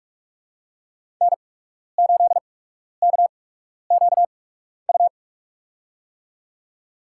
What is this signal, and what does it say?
Morse code, heard as a keyed tone, about 46 wpm, spaced as for 12 wpm, 700 Hz. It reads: N8KQU
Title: USA Call Signs - 5 Characters - 8x Character Spacing - 45wpm